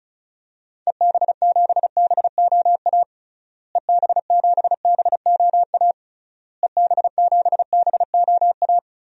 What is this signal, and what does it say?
Morse code, heard as a keyed tone, about 34 wpm, 700 Hz. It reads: EB7BOA EB7BOA EB7BOA
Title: DXCC-100 Call Signs - 6 Characters - Repeated 3x - 35wpm